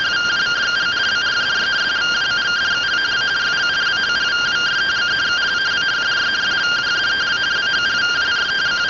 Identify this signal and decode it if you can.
Two-tone teleprinter signal, about 170 Hz shift.
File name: findings / axial iodine 10mm 184.gen-15 rtty rx